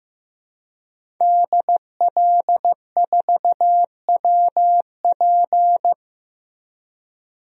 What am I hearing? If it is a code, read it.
Morse code, heard as a keyed tone, about 15 wpm, 700 Hz. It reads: DL4WP
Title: DXCC-100 Call Signs - 5 Characters - Repeated 2x - 15wpm